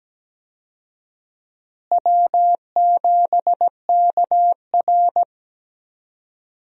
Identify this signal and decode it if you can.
Morse code, heard as a keyed tone, about 17 wpm, 700 Hz. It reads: W7KR
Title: USA Call Signs - 4 Characters - Repeated 2x - 17wpm